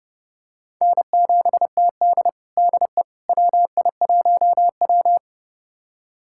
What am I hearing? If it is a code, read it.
Morse code, heard as a keyed tone, about 31 wpm, 700 Hz. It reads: N7TD DE WI1W
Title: USA Callsign DE Callsign - 4 Characters - 30wpm